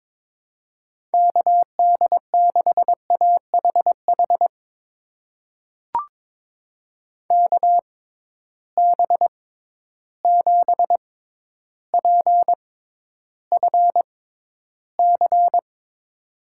Morse code, heard as a keyed tone, about 22 wpm, spaced as for 11 wpm, 700 Hz. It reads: KD6AHH KB7PFC
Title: USA Call Signs - 6 Characters - 6x Character Spacing - 22wpm